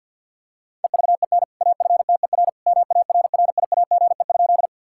Morse code, heard as a keyed tone, about 50 wpm, 700 Hz. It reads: EVEN AFTER MARRIAGE?
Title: Sentences from Top 1000 Words - Review Part 5 - 50wpm